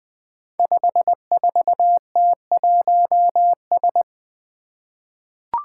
Morse code, heard as a keyed tone, about 20 wpm, 700 Hz. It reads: 54T1S
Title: Five Challenging Letter-Number Combinations 20wpm